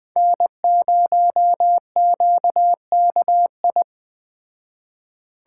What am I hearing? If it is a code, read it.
Morse code, heard as a keyed tone, about 20 wpm, 700 Hz. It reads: N0QKI
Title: Speed Racing - USA 1x3 Call Signs 17wpm